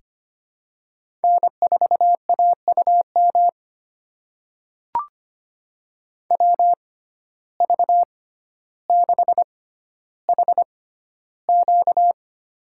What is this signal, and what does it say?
Morse code, heard as a keyed tone, about 25 wpm, spaced as for 13 wpm, 700 Hz. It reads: N4AUM WV6HQ
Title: USA Call Signs - 5 Characters - 6x Character Spacing - 25wpm